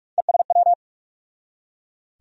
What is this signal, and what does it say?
Morse code, heard as a keyed tone, about 41 wpm, 700 Hz. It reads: ESW
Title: USA Call Sign Postfixes - 3 Letters 45wpm